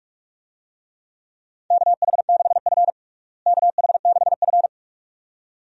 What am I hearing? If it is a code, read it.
Morse code, heard as a keyed tone, about 44 wpm, 700 Hz. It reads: KH6F KH6F
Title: DXCC-100 Call Signs - 4 Characters - Repeated 2x - 45wpm